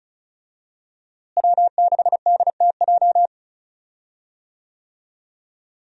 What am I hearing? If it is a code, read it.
Morse code, heard as a keyed tone, about 34 wpm, 700 Hz. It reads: W6DTJ
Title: USA Call Signs - 5 Characters - Repeated 2x - 35wpm